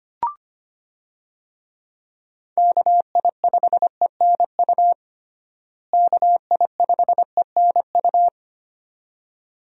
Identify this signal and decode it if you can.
Morse code, heard as a keyed tone, about 25 wpm, 700 Hz. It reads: KI5ENU KI5ENU
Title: USA Call Signs - 6 Characters - Repeated 2x - 25wpm